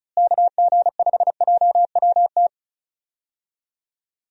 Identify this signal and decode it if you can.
Morse code, heard as a keyed tone, about 36 wpm, 700 Hz. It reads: KG5JWT
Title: USA Call Signs - 6 Characters - Repeated 3x - 35wpm